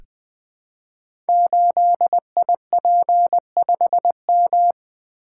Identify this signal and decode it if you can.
Morse code, heard as a keyed tone, about 20 wpm, 700 Hz. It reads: 8IP5M
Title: Code Groups - 5 Letter:Numbers - 20wpm